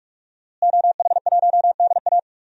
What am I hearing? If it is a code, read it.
Morse code, heard as a keyed tone, about 46 wpm, 700 Hz. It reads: OH1DA
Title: DXCC-100 Call Signs - 5 Characters - Repeated 2x - 45wpm